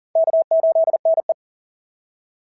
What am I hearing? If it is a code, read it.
Morse code, heard as a keyed tone, about 40 wpm, 690 Hz. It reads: K8NE
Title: SOTA QSO Practice 40wpm